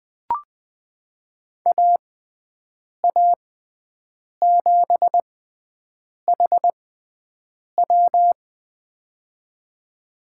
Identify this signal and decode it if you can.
Morse code, heard as a keyed tone, about 20 wpm, spaced as for 7 wpm, 700 Hz. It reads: AA7HW
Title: USA Call Signs - 5 Characters - 6x Character Spacing - 20wpm